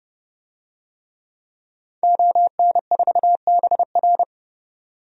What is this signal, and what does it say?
Morse code, heard as a keyed tone, about 32 wpm, 700 Hz. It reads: ON4BR
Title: DXCC-100 Call Signs - 5 Characters - 30wpm